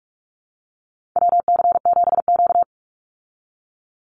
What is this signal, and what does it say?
Morse code, heard as a keyed tone, about 44 wpm, 700 Hz. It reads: WC7Q
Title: Call Signs Repeated 2x - 45wpm